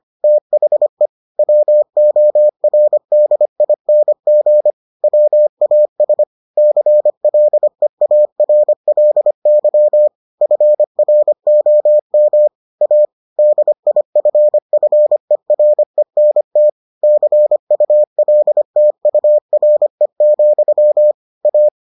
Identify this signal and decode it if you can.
Morse code, heard as a keyed tone, about 25 wpm, 600 Hz. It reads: THE WORDING WAS CLEARLY FROM A DIFFERENT CULTURE, A